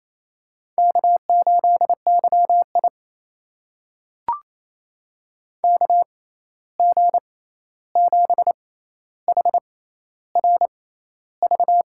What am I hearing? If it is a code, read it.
Morse code, heard as a keyed tone, about 28 wpm, spaced as for 13 wpm, 700 Hz. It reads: K8YI KG7HRV
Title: USA Call Signs - 6x Character Spacing - 28wpm